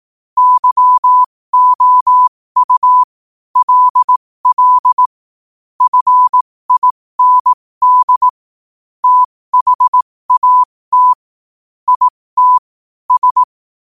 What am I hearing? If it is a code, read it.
Morse code, heard as a keyed tone, about 18 wpm, spaced as for 15 wpm, 1000 Hz. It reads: YOU LL FIND THAT IT S
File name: qotdfiles / 15wpm